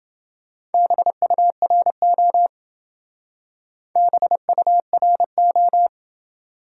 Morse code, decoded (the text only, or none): BURO BURO